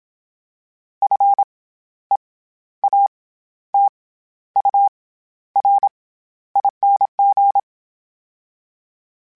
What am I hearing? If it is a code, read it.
Morse code, heard as a keyed tone, about 27 wpm, 800 Hz. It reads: F E A T U R ING
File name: Special INGs-mp3s / Special INGs-mp3s / SlowToFaster ing @25